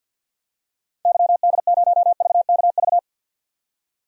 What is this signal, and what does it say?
Morse code, heard as a keyed tone, about 52 wpm, 700 Hz. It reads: YD0VKV
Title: DXCC-100 Call Signs - 6 Characters - 50wpm